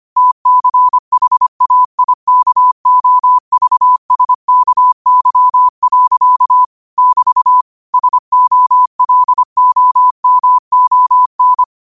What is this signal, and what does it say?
Morse code, heard as a keyed tone, about 25 wpm, 1000 Hz. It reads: TCHAIKOVSKY. <BT> SOLOMON